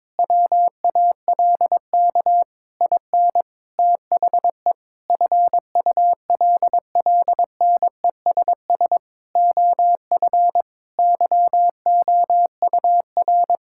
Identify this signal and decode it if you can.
Morse code, heard as a keyed tone, about 22 wpm, 700 Hz. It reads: WALK IN THE FULLNESS OF YOUR
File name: Sets of 9 Words - 22wpm